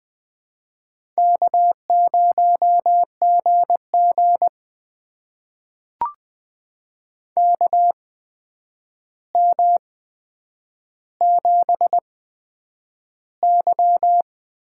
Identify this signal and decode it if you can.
Morse code, heard as a keyed tone, about 20 wpm, spaced as for 9 wpm, 700 Hz. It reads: K0GG KM7Y
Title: USA Call Signs - 4 Characters - 8x Character Spacing - 20wpm